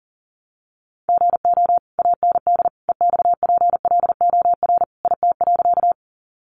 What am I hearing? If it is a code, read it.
Morse code, heard as a keyed tone, about 40 wpm, 700 Hz. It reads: GO AND EXPLOR IT.
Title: Sentences from Top 1100 Words 40wpm